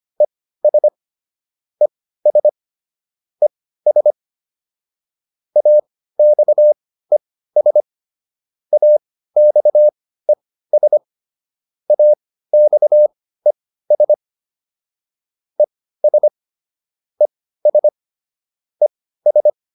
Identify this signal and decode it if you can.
Morse code, heard as a keyed tone, about 25 wpm, spaced as for 15 wpm, 600 Hz. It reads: ES ES ES AXES AXES AXES ES ES ES